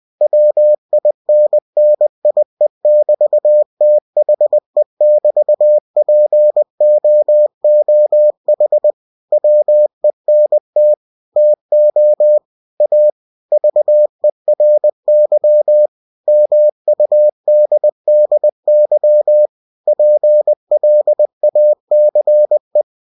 Morse code, heard as a keyed tone, about 20 wpm, 600 Hz. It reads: WINNIE<BT>THE<BT>POOH WENT TO A VERY MUDDY PLACE
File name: cw-books / winnie-the-pooh / winnie-20wpm-page-0006